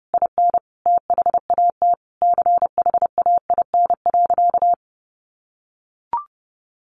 Morse code, heard as a keyed tone, about 28 wpm, 700 Hz. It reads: IN THAT CHAIN.
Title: Sentences from Top 1500 Words 30wpm